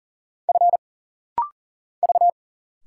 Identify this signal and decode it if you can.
Morse code, heard as a keyed tone, about 42 wpm, 700 Hz. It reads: F V
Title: Single Letters - Rapid-Fire 40wpm